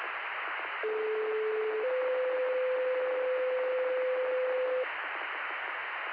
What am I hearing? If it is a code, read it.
Two-tone page, heard as a one-second tone, then a three-second tone, about 430 then 510 Hz, with noise. It is heard from 0.8 to 4.8 s.
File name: Lakeland EMS Gen